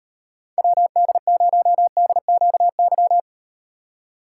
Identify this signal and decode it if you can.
Morse code, heard as a keyed tone, about 38 wpm, 700 Hz. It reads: WD0DQY